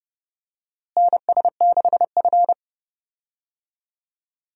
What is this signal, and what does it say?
Morse code, heard as a keyed tone, about 32 wpm, 700 Hz. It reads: NS6F
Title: USA Call Signs - 4 Characters - Repeated 3x - 30wpm